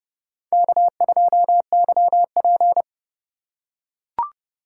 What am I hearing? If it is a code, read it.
Morse code, heard as a keyed tone, about 30 wpm, 700 Hz. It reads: K2YP
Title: USA Call Signs - 4 Characters - 3x Character Spacing - 30wpm